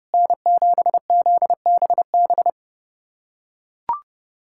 Morse code, heard as a keyed tone, about 30 wpm, 700 Hz. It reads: N7ZBB
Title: USA Callsign DE Callsign - 5 Characters - 30wpm